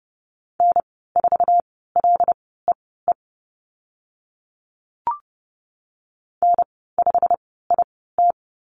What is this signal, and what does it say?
Morse code, heard as a keyed tone, about 30 wpm, spaced as for 17 wpm, 700 Hz. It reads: N4LEE N5IT
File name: Lesson 15g - Number 5 - Call Signs 30wpm